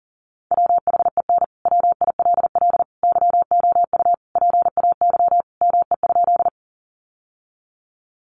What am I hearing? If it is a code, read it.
Morse code, heard as a keyed tone, about 39 wpm, 700 Hz. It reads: WHEN WILL YOU PAY ME?
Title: Sentences from Top 700 Words 40wpm